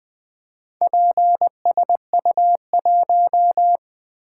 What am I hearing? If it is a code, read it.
Morse code, heard as a keyed tone, about 20 wpm, 700 Hz. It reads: PSU1
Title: Four Letter-Number Combinations - 20wpm